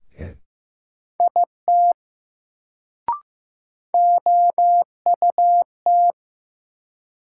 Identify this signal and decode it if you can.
Morse code, heard as a keyed tone, about 15 wpm, 700 Hz. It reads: IT OUT
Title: Sentences from Top 100 Words - New Word Drill - 15wpm